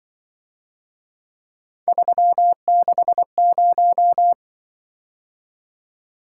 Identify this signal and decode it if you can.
Morse code, heard as a keyed tone, about 24 wpm, 700 Hz. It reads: 360
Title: Speed-Racing - Three Letter-Number Combinations - 20wpm